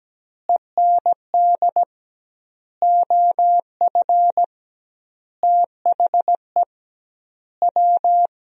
Morse code, heard as a keyed tone, about 17 wpm, 700 Hz. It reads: END OF THE W
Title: 2x Word-Spacing - Sentences from Top 200 Words - 17wpm